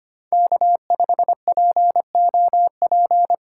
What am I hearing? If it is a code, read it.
Morse code, heard as a keyed tone, about 25 wpm, 700 Hz. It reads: K5POP
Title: USA Call Signs - 5 Characters - 4x Character Spacing - 25wpm